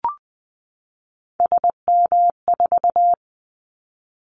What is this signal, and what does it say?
Morse code, heard as a keyed tone, about 20 wpm, 700 Hz. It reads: SM4
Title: Callsign Prefixes for Top 25 DXCC - 20wpm